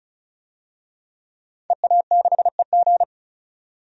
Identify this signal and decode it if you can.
Morse code, heard as a keyed tone, about 34 wpm, 700 Hz. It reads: EA6EG
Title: DXCC-100 Call Signs - 5 Characters - 35wpm